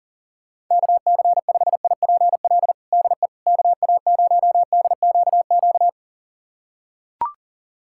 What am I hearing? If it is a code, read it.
Morse code, heard as a keyed tone, about 40 wpm, 700 Hz. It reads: KC5IPL DE KA0DQQ
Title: USA Callsign DE Callsign - 6 Characters - 40wpm